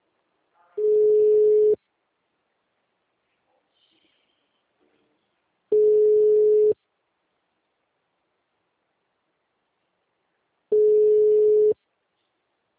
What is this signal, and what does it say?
Ringback tone (Europe, 425 Hz), with speech.
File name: out- -2001- - - .3336